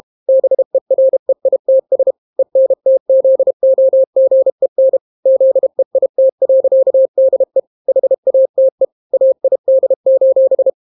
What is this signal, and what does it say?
Morse code, heard as a keyed tone, about 32 wpm, 530 Hz. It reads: BEREITS ENTZOGEN ZEIT.DE HATE AID: